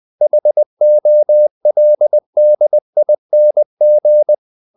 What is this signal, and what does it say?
Morse code, heard as a keyed tone, about 20 wpm, 600 Hz. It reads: HOLDING
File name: the old man and the sea part 2 20wpm